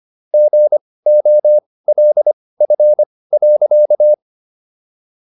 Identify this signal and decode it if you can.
Morse code, heard as a keyed tone, about 25 wpm, spaced as for 18 wpm, 600 Hz. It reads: GOLF.